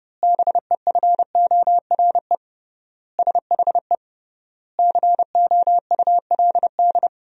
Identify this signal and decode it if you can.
Morse code, heard as a keyed tone, about 32 wpm, 700 Hz. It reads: BEFORE SHE COULD
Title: 3x Word-Spacing - Sentences from Top 500 Words - 30wpm